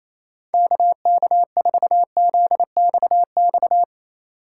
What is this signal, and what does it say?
Morse code, heard as a keyed tone, about 28 wpm, 700 Hz. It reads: KK4ZXX
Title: USA Call Signs - 6 Characters - 6x Character Spacing - 28wpm